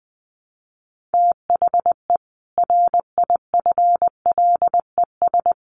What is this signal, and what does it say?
Morse code, heard as a keyed tone, about 20 wpm, 700 Hz. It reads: THE RIFLES
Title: A Sound Of Thunder 20wpm